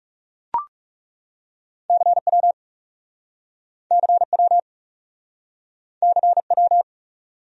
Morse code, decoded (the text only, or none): CW CW CW